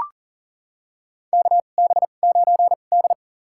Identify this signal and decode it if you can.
Morse code, heard as a keyed tone, about 40 wpm, spaced as for 29 wpm, 700 Hz. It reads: KB9D
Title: USA Call Signs - 4 Characters - 2x Character Spacing - 40wpm